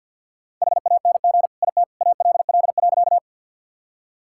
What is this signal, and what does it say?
Morse code, heard as a keyed tone, about 50 wpm, 700 Hz. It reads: HANG IT ALL.
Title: Sentences from Top 1600 Words 50wpm